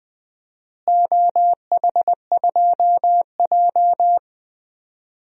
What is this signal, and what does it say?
Morse code, heard as a keyed tone, about 20 wpm, 700 Hz. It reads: OH2J
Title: DXCC-100 Call Signs - 4 Characters - Repeated 3x - 20wpm